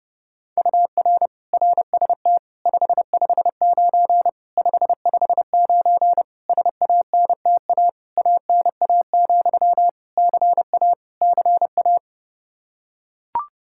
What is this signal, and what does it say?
Morse code, heard as a keyed tone, about 30 wpm, 700 Hz. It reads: UR RST 559 559 SANTA ANA, CA CA
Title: Practice QSO - Exchange 1 - Beginning - 30wpm